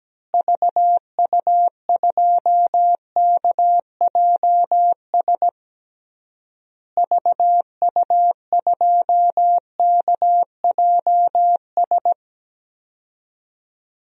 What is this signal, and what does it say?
Morse code, heard as a keyed tone, about 17 wpm, 700 Hz. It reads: VU2KJS VU2KJS
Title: DXCC-100 Call Signs - 6 Characters - Repeated 3x - 17wpm